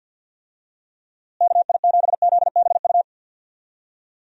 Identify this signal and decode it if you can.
Morse code, heard as a keyed tone, about 47 wpm, 700 Hz. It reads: KI7ZBU
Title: USA Call Signs - 6 Characters - Repeated 3x - 50wpm